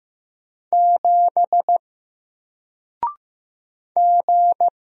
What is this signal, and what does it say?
Morse code, heard as a keyed tone, about 15 wpm, 700 Hz. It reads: 7 G
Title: Single Letter-Number Alt 15wpm